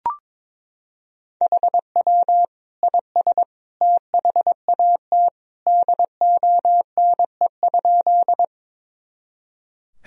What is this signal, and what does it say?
Morse code, heard as a keyed tone, about 22 wpm, 700 Hz. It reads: HW IS THAT DONE?
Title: Sentences from Top 500 Words and Abbreviations 22wpm